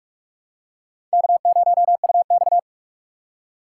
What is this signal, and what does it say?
Morse code, heard as a keyed tone, about 44 wpm, 700 Hz. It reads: K0UX
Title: USA Call Signs - 4 Characters - 6x Character Spacing - 45wpm